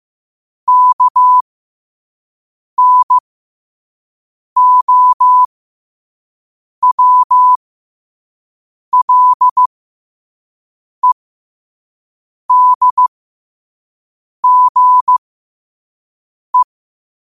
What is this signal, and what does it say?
Morse code, heard as a keyed tone, about 15 wpm, spaced as for 5 wpm, 1000 Hz. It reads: KNOWLEDGE
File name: qotdfiles / 5wpm